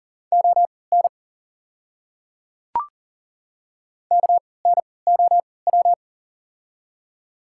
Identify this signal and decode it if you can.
Morse code, heard as a keyed tone, about 42 wpm, spaced as for 23 wpm, 690 Hz. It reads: ON KNOW
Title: 3x Character Spacing - Top 100 Words - 40wpm